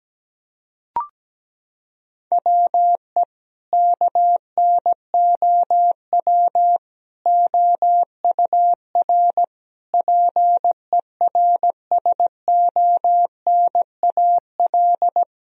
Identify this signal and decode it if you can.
Morse code, heard as a keyed tone, about 17 wpm, 700 Hz. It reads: WE KNOW OUR PERSONAL
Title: Sentences from Top 1000 Words - Review Part 4 - 17wpm